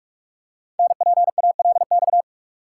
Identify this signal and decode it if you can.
Morse code, heard as a keyed tone, about 44 wpm, 700 Hz. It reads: NPALX
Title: Code Groups - 5 Letter:Numbers - 45wpm